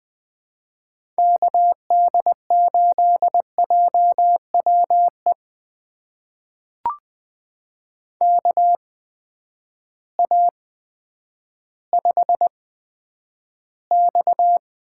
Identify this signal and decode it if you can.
Morse code, heard as a keyed tone, about 20 wpm, spaced as for 10 wpm, 700 Hz. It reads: KD8JWE KA5X
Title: USA Call Signs - 8x Character Spacing - 20wpm